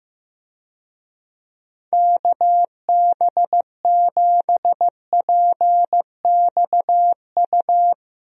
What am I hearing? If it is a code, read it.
Morse code, heard as a keyed tone, about 15 wpm, 700 Hz. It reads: KB7PXU
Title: USA Call Signs - 6 Characters - Repeated 3x - 15wpm